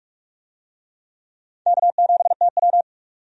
Morse code, heard as a keyed tone, about 46 wpm, 700 Hz. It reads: K7TW
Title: USA Call Signs - 4 Characters - 45wpm